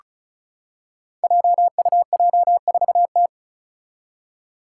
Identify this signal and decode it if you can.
Morse code, heard as a keyed tone, about 36 wpm, 700 Hz. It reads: JUJ4T
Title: Five Challenging Letter-Number Combinations 35wpm